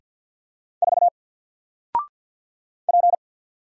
Morse code, heard as a keyed tone, about 50 wpm, 700 Hz. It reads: 4 P
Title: Single Character (Letter:Number:Punctuation:Prosign) 50wpm